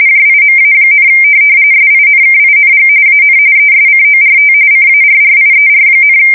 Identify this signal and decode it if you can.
RTTY, about 45.45 baud. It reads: JIMA, SINKING AN ESCORT CARRIER,
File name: fsk170 / USS-New-Jersey